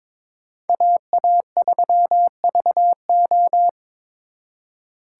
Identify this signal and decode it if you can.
Morse code, heard as a keyed tone, about 26 wpm, 700 Hz. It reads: AA3VO
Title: Speed Racing - USA 2x2 Call Signs 22wpm